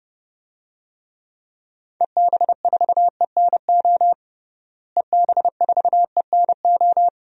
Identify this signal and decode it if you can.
Morse code, heard as a keyed tone, about 32 wpm, 700 Hz. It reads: EB4ENO EB4ENO
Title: DXCC-100 Call Signs - 6 Characters - Repeated 2x - 30wpm